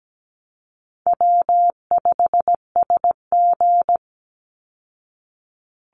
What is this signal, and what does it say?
Morse code, heard as a keyed tone, about 17 wpm, 700 Hz. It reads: W5SG
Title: Easy Call Signs Repeated 2x - 17wpm